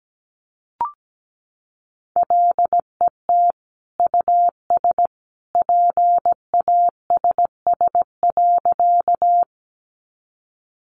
Morse code, heard as a keyed tone, about 17 wpm, 700 Hz. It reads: LET US PASS.
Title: Sentences from Top 400 Words 17wpm